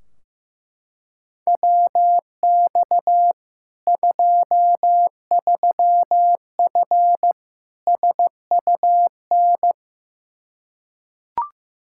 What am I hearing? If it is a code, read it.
Morse code, heard as a keyed tone, about 15 wpm, 700 Hz. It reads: WX 23F SUN
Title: Practice QSO - Exchange 2 - Info - 15wpm